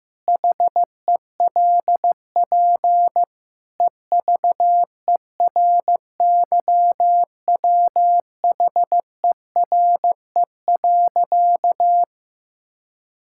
Morse code, heard as a keyed tone, about 15 wpm, 700 Hz. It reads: HELP EVERYWHERE.
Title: Sentences from Top 1600 Words 15wpm